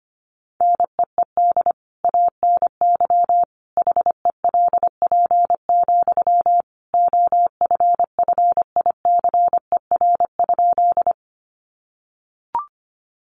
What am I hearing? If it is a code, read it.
Morse code, heard as a keyed tone, about 25 wpm, 700 Hz. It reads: NEED ANY HELP, OFFICER?
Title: Sentences from Top 1200 Words 25wpm